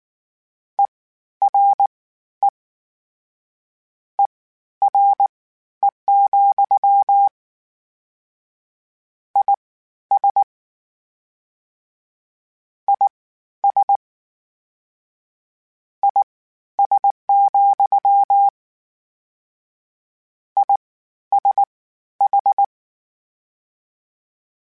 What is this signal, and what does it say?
Morse code, heard as a keyed tone, about 19 wpm, spaced as for 12 wpm, 800 Hz. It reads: ERE ERE, IS IS IS, ISH
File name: General / General-mp3s / Suffixes sp @18